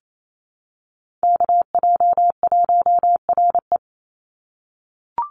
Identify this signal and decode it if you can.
Morse code, heard as a keyed tone, about 28 wpm, 700 Hz. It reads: KJ1RE
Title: Call Signs - 28wpm